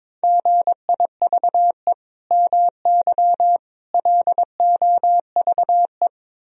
Morse code, heard as a keyed tone, about 22 wpm, 700 Hz. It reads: GIVE MY LOVE
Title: Sentences from Top 1000 Words - Review Part 4 - 22wpm